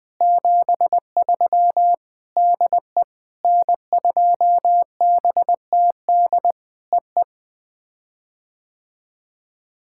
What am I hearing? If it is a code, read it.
Morse code, heard as a keyed tone, about 19 wpm, 700 Hz. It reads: 73 DE N2BTD EE
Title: POTA QSO Practice - 20wpm